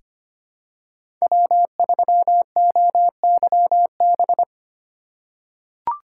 Morse code, heard as a keyed tone, about 25 wpm, 700 Hz. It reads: W3OYB
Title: USA Call Signs - 5 Characters - 4x Character Spacing - 25wpm